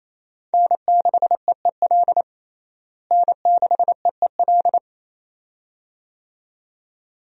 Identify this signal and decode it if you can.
Morse code, heard as a keyed tone, about 28 wpm, 700 Hz. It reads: N6EEL N6EEL
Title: USA Call Signs - 5 Characters - Repeated 2x - 28wpm